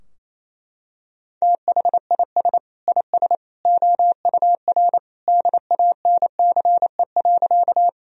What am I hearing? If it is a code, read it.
Morse code, heard as a keyed tone, about 28 wpm, 700 Hz. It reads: THIS IS OUR DANCE.
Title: Sentences from Top 1000 Words - Review Part 4 - 28wpm